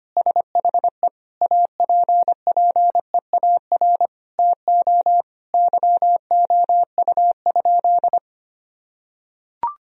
Morse code, heard as a keyed tone, about 25 wpm, 700 Hz. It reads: SHE APPEAR TO YOU?
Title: Sentences from Top 1000 Words - Review Part 2 - 25wpm